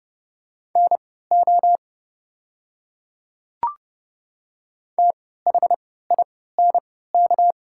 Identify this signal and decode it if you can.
Morse code, heard as a keyed tone, about 29 wpm, spaced as for 17 wpm, 690 Hz. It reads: NO THINK
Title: 3x Character Spacing - Top 200 Words - 30wpm